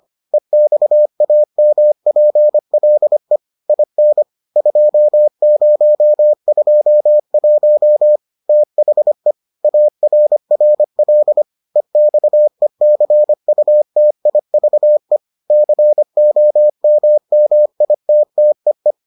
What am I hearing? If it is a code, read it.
Morse code, heard as a keyed tone, about 25 wpm, 600 Hz. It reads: EXAMPLE IN 2021 THE ARRL EXECUTIVE COMMITTEE